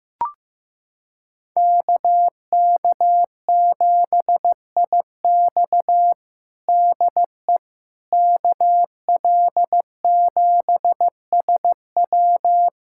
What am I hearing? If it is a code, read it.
Morse code, heard as a keyed tone, about 15 wpm, 700 Hz. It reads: KK7IX DE KL7SW
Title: USA Callsign DE Callsign - 5 Characters - 15wpm